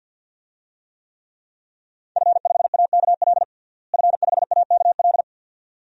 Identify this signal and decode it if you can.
Morse code, heard as a keyed tone, about 46 wpm, 700 Hz. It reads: F5AKL F5AKL
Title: DXCC-100 Call Signs - 5 Characters - Repeated 2x - 50wpm